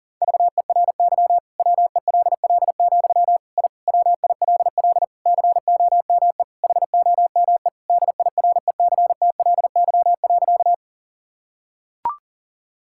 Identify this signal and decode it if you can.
Morse code, heard as a keyed tone, about 40 wpm, 700 Hz. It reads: VERY WELL, I WILL COME HOME DIRECTLY.
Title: Sentences from Top 1000 Words - Review Part 5 - 40wpm